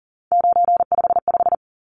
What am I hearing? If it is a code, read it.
Morse code, heard as a keyed tone, about 40 wpm, 700 Hz. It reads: 955
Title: Numbers - 3-Digit 40wpm